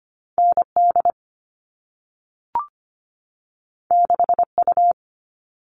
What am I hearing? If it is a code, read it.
Morse code, heard as a keyed tone, about 25 wpm, 700 Hz. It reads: ND 6U
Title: All Cut Numbers - 2-Digit 25wpm